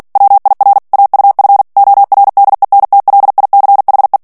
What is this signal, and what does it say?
Morse code, heard as a keyed tone, about 48 wpm, 800 Hz. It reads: WIR AUF ORDENTLICHE
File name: morse mp3 / ALLEMAND texte clair 2 vitesse 45 mots minute